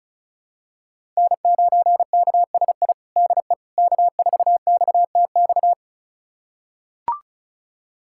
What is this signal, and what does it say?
Morse code, heard as a keyed tone, about 36 wpm, 700 Hz. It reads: N9KSI DE K4XTX
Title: USA Callsign DE Callsign - 5 Characters - 35wpm